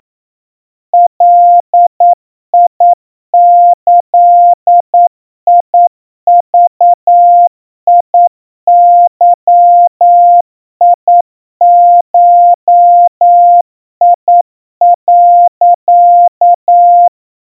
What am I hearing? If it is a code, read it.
Morse code (Japanese Wabun code), heard as a keyed tone, about 9 wpm, 700 Hz. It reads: ガギグゲゴ、
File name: kawa / Morse sample / iroha 30 cpm 700 Hz Crispy